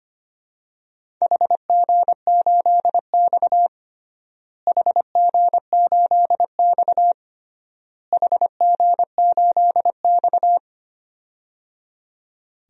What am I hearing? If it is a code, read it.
Morse code, heard as a keyed tone, about 25 wpm, 700 Hz. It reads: HG8X HG8X HG8X
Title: DXCC-100 Call Signs - 4 Characters - Repeated 3x - 25wpm